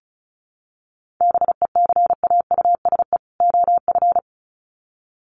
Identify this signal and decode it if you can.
Morse code, heard as a keyed tone, about 34 wpm, 700 Hz. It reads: BECAUSE OF
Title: Sets of 2 Words 35wpm